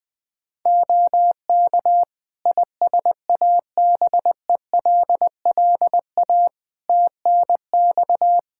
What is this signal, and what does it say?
Morse code, heard as a keyed tone, about 20 wpm, 700 Hz. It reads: OK ISABELLA TNX
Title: Practice QSO - Exchange 4 - Ending - 20wpm